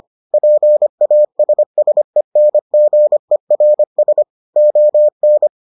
Morse code, heard as a keyed tone, about 25 wpm, 600 Hz. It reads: PASSENGERS ON